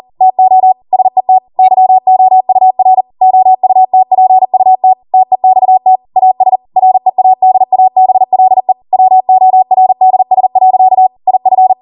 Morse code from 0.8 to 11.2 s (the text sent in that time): SET YOUR OUTPUT TEXT AS READABLE WORDS.